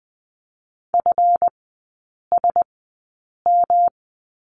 Morse code, heard as a keyed tone, about 20 wpm, spaced as for 8 wpm, 700 Hz. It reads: FSM
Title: 191 Lesson 31e - Letter v - Three Characters 20wpm